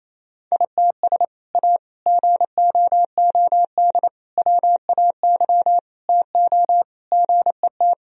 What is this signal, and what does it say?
Morse code, heard as a keyed tone, about 28 wpm, 690 Hz. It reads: ITS A GOOD WAY TO GET
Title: Sentences from Top 1000 Words - Review Part 3 - 28wpm